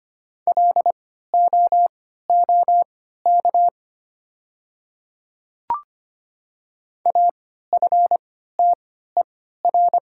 Morse code, heard as a keyed tone, about 25 wpm, spaced as for 14 wpm, 690 Hz. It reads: LOOK AFTER
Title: 3x Character Spacing - Top 500 Words - 25wpm